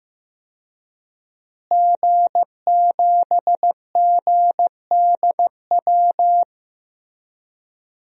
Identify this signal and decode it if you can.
Morse code, heard as a keyed tone, about 16 wpm, 700 Hz. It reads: G7GDW
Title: DXCC-100 Call Signs - 5 Characters - Repeated 2x - 15wpm